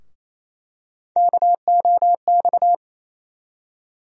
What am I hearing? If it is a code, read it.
Morse code, heard as a keyed tone, about 28 wpm, 700 Hz. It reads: KOX
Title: USA Call Sign Postfixes - 3 Letters 28wpm